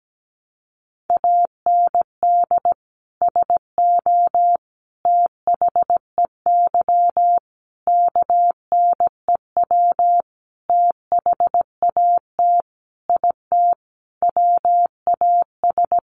Morse code, heard as a keyed tone, about 17 wpm, 700 Hz. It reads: AND SO THEY KNEW THAT IT WAS
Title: Sentences from Top 400 Words 17wpm